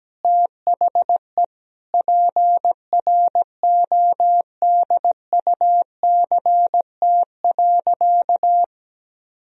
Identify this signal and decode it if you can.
Morse code, heard as a keyed tone, about 17 wpm, 700 Hz. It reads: THE PRODUCT.
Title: Sentences from Top 1000 Words - Review Part 2 - 17wpm